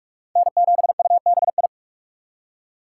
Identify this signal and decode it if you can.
Morse code, heard as a keyed tone, about 41 wpm, 700 Hz. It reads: N7UBI